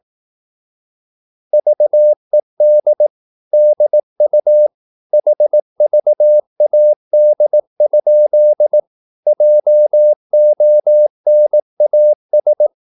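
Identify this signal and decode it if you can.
Morse code, heard as a keyed tone, about 18 wpm, 600 Hz. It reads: VED DU HVAD? JONAS
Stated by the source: Slow (QRS) Morse practice at 18 wpm, CW med Gnister.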